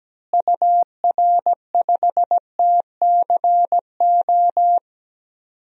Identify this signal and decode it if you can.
Morse code, heard as a keyed tone, about 17 wpm, 700 Hz. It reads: UR5TCO
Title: DXCC-100 Call Signs - 6 Characters - Repeated 2x - 17wpm